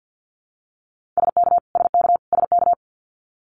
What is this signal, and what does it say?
Morse code, heard as a keyed tone, about 50 wpm, 700 Hz. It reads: SK SK SK
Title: Canadian Provinces and Territories - Repeated 3x - 50wpm